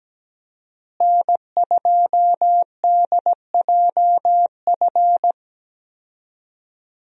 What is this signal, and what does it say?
Morse code, heard as a keyed tone, about 17 wpm, 700 Hz. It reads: N2DJF
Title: USA Call Signs - 5 Characters - Repeated 3x - 17wpm